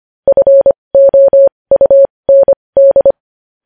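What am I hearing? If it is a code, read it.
Morse code, heard as a keyed tone, about 25 wpm, spaced as for 20 wpm, 550 Hz. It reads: FOUND
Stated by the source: Morse code at 20 wpm.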